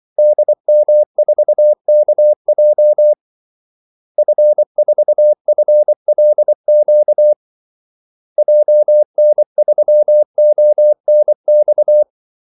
Morse code, heard as a keyed tone, about 24 wpm, 600 Hz. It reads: DM4KJ F4FLQ JN3ONX
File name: Callsigns-24wpm